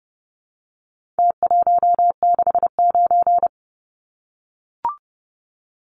Morse code, heard as a keyed tone, about 30 wpm, 700 Hz. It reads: T169
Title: Numbers - Contest Style - 4-Digit 30wpm